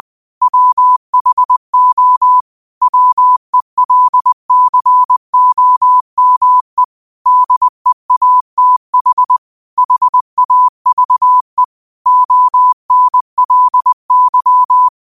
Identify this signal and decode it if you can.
Morse code, heard as a keyed tone, about 20 wpm, 1000 Hz. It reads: WHO WELCOME DEATH HAVE ONLY
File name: qotdfiles / 20wpm